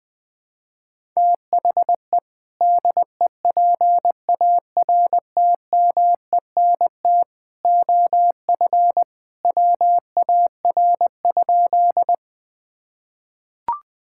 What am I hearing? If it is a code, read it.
Morse code, heard as a keyed tone, about 21 wpm, 700 Hz. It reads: THE DEPARTMENT OF WAR?
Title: Sentences from Top 1000 Words - Review Part 5 - 20wpm